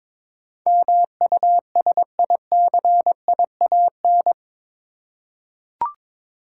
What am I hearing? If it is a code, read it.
Morse code, heard as a keyed tone, about 21 wpm, 700 Hz. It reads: MUSICIAN
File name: practice / 22wpm / Eight Letter Words - 22wpm